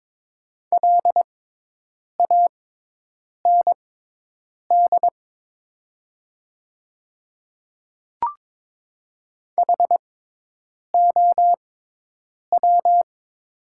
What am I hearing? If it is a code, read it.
Morse code, heard as a keyed tone, about 22 wpm, spaced as for 8 wpm, 700 Hz. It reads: LAND HOW
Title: 6x Character Spacing - Top 200 Words - 22wpm